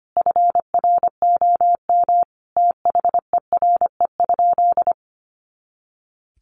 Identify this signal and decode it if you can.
Morse code, heard as a keyed tone, about 25 wpm, 700 Hz. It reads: FROM THERE?
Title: Sentences from Top 100 Words 25wpm